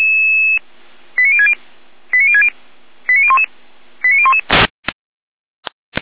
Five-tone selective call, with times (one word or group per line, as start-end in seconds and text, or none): 1.2-1.5 89060
2.1-2.5 89060
3.1-3.5 89011
4.0-4.4 89011